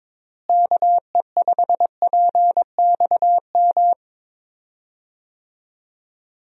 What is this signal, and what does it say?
Morse code, heard as a keyed tone, about 22 wpm, 700 Hz. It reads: KE5PXM
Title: USA Call Signs - 6 Characters - Repeated 3x - 22wpm